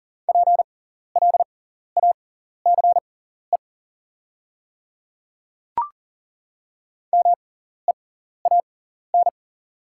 Morse code, heard as a keyed tone, about 40 wpm, spaced as for 14 wpm, 690 Hz. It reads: PLACE MEAN
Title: 6x Character Spacing - Top 300 Words - 40wpm